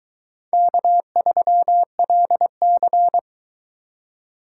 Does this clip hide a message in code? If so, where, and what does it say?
Yes: Morse code from 0.5 to 3.2 s, heard as a keyed tone, about 22 wpm, 700 Hz. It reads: K3LC